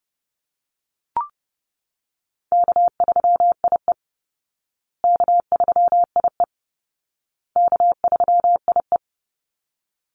Morse code, heard as a keyed tone, about 30 wpm, 700 Hz. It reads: K3IE K3IE K3IE
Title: Easy Call Signs Repeated 3x - 30wpm